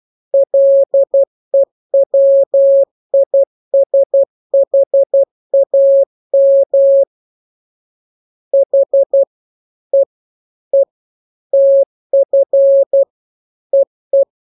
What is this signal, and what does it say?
Morse code, heard as a keyed tone, about 12 wpm, 550 Hz. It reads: LEWISHAM H E E TF EE
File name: jokes / joke 15 12 0000